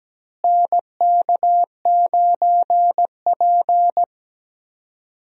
Morse code, heard as a keyed tone, about 17 wpm, 700 Hz. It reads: NK9P
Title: USA Call Signs - 4 Characters - Repeated 2x - 17wpm